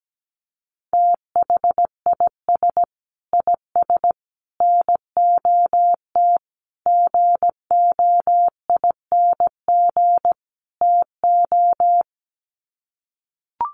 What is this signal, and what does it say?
Morse code, heard as a keyed tone, about 17 wpm, 700 Hz. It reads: THIS IS NOT GOING TO
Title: Sets of 5 Words Encore2 17wpm